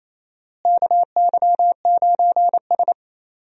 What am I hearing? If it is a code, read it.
Morse code, heard as a keyed tone, about 28 wpm, 700 Hz. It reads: KY9S
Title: USA Call Signs - 4 Characters - Repeated 2x - 28wpm